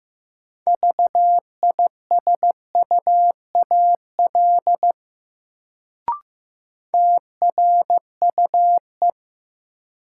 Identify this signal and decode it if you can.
Morse code, heard as a keyed tone, about 15 wpm, 700 Hz. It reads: VISUAL TRUE
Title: Sets of 5 Words Encore - New Word Drill - 15wpm